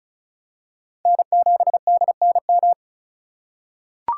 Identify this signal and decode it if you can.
Morse code, heard as a keyed tone, about 34 wpm, 700 Hz. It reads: N7DNM
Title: USA Call Signs - 5 Characters - 4x Character Spacing - 35wpm